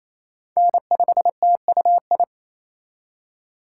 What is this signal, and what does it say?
Morse code, heard as a keyed tone, about 28 wpm, 700 Hz. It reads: N5TUI
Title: USA Call Signs - 2x Character Spacing - 28wpm